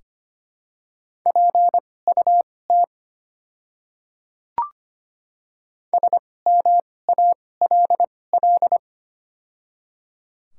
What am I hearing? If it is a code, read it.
Morse code, heard as a keyed tone, about 25 wpm, spaced as for 18 wpm, 700 Hz. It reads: PUT SMALL
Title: 2x Character Spacing - Top 200 Words - 25wpm